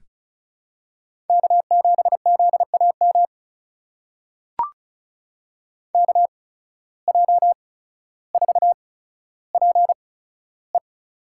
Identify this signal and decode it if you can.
Morse code, heard as a keyed tone, about 34 wpm, spaced as for 15 wpm, 700 Hz. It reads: K7ZAM KJ4PE
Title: USA Call Signs - 5 Characters - 8x Character Spacing - 35wpm